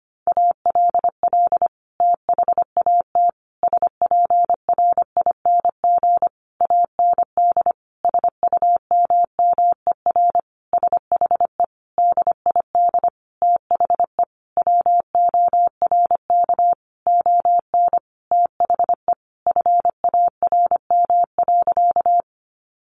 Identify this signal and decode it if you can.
Morse code, heard as a keyed tone, about 25 wpm, 700 Hz. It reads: ALL THAT SPRING AND SUMMER SHE DID THE WORK ON THE FARM.
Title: Sentences from Top 1200 Words 25wpm